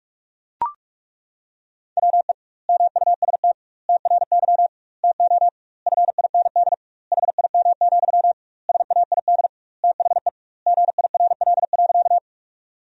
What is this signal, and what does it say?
Morse code, heard as a keyed tone, about 42 wpm, 700 Hz. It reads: WE MUST TRY TO FIND HIM, SAID THE GIRL.
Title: 2x Word-Spacing - Sentences from Top 400 Words - 45wpm